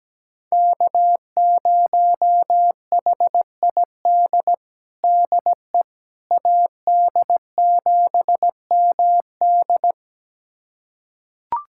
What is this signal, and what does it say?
Morse code, heard as a keyed tone, about 17 wpm, 700 Hz. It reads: K0HID DE AD7MD
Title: USA Callsign DE Callsign - 5 Characters - 17wpm